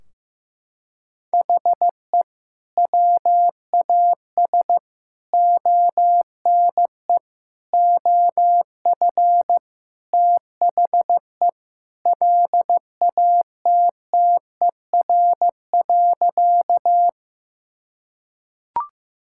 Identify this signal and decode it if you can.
Morse code, heard as a keyed tone, about 16 wpm, 700 Hz. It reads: HE WAS ONE OF THE LATTER.
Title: Sentences from Top 1600 Words 15wpm